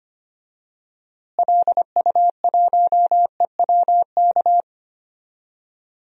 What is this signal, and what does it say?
Morse code, heard as a keyed tone, about 25 wpm, 700 Hz. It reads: LU1EWK
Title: DXCC-100 Call Signs - 6 Characters - Repeated 3x - 25wpm